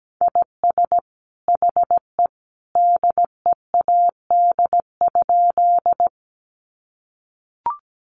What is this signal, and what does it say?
Morse code, heard as a keyed tone, about 17 wpm, 700 Hz. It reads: IS HE DEAD?
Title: Sentences from Top 1500 Words 17wpm